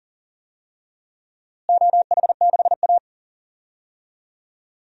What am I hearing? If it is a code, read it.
Morse code, heard as a keyed tone, about 40 wpm, 700 Hz. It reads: OH6A
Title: DXCC-100 Call Signs - 4 Characters - Repeated 2x - 40wpm